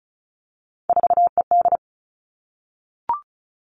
Morse code, heard as a keyed tone, about 36 wpm, 700 Hz. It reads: 4ED
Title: Three Challenging Letter-Number Combinations 35wpm